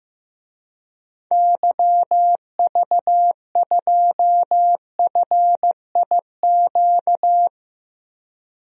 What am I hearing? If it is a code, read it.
Morse code, heard as a keyed tone, about 15 wpm, 700 Hz. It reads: YV2FIQ
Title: DXCC-100 Call Signs - 6 Characters - 15wpm